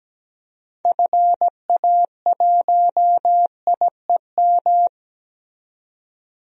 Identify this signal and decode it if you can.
Morse code, heard as a keyed tone, about 17 wpm, 700 Hz. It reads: FA1IEM
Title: DXCC-100 Call Signs - 6 Characters - 17wpm